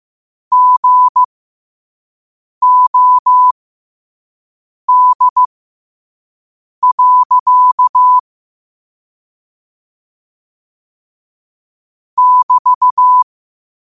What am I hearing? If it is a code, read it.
Morse code, heard as a keyed tone, about 15 wpm, spaced as for 5 wpm, 1000 Hz. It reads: GOD. <BT>
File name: qotdfiles / 5wpm